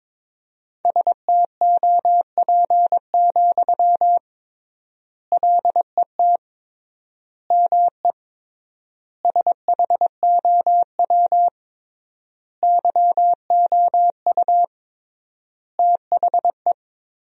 Morse code, heard as a keyed tone, about 21 wpm, 700 Hz. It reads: STOP, LET ME SHOW YOU THE
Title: 3x Word-Spacing - Sentences from Top 300 Words - 22wpm